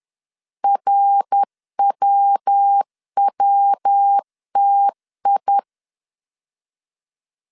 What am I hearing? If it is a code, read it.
Morse code, heard as a keyed tone, about 10 wpm, 800 Hz. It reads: RWWTI
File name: CW / DET / 1 3 1500C S C 10wpm 3335